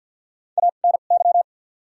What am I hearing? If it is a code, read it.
Morse code, heard as a keyed tone, about 53 wpm, spaced as for 37 wpm, 700 Hz. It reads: ANY
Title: 2x Character Spacing - Top 500 Words - 50wpm